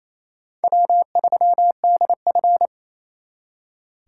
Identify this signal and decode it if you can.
Morse code, heard as a keyed tone, about 36 wpm, 700 Hz. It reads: W3DF